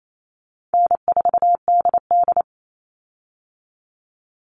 Morse code, heard as a keyed tone, about 28 wpm, 700 Hz. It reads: N4DD
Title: Easy Call Signs Repeated 2x - 28wpm